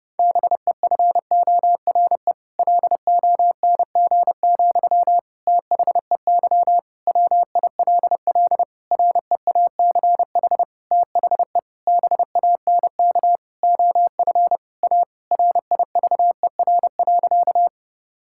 Morse code, heard as a keyed tone, about 30 wpm, 700 Hz. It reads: BEFORE LONG, THEY WILL REACH THE BANK OF A RIVER.
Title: Sentences from Top 1000 Words - Review Part 6 - 30wpm